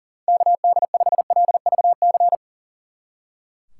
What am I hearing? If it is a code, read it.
Morse code, heard as a keyed tone, about 38 wpm, 700 Hz. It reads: KD5LVC
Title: Speed Racing - USA 2x3 Call Signs 40wpm